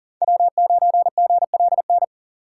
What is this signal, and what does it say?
Morse code, heard as a keyed tone, about 37 wpm, 700 Hz. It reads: W9GLN
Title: USA Call Signs - 5 Characters - 4x Character Spacing - 40wpm